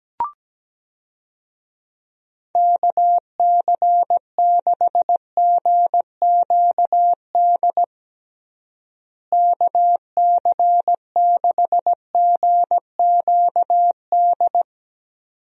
Morse code, heard as a keyed tone, about 17 wpm, 700 Hz. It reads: KC6GQD KC6GQD
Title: USA Call Signs - 6 Characters - Repeated 2x - 17wpm